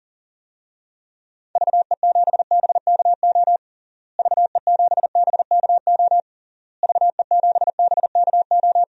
Morse code, heard as a keyed tone, about 40 wpm, 700 Hz. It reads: VE7BKO VE7BKO VE7BKO
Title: DXCC-100 Call Signs - 6 Characters - Repeated 3x - 40wpm